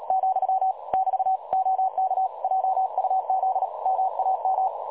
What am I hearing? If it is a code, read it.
Morse code, heard as a keyed tone, about 35 wpm, 760 Hz. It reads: , <BT> OK JUZ MAM
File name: dlazaawansowanych / nagraniazpasma / websdr recording 3565.0kHz